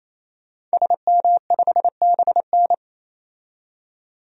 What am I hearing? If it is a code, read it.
Morse code, heard as a keyed tone, about 28 wpm, 700 Hz. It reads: SM5BN
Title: DXCC-100 Call Signs - 5 Characters - 28wpm